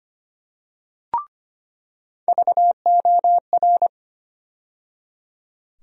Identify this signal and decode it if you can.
Morse code, heard as a keyed tone, about 25 wpm, 700 Hz. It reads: VOR
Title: USA Call Sign Postfixes - 3 Letters 25wpm